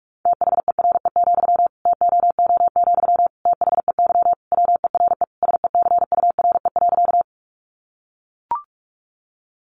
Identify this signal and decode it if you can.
Morse code, heard as a keyed tone, about 44 wpm, 700 Hz. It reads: THERE, TOO, THEY WERE SECURE.
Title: Sentences from Top 1200 Words 45wpm